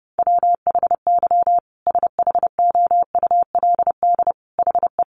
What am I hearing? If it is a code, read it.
Morse code, heard as a keyed tone, about 30 wpm, 700 Hz. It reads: WHY SHOULD HE
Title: Sentences from Top 1300 Words 30wpm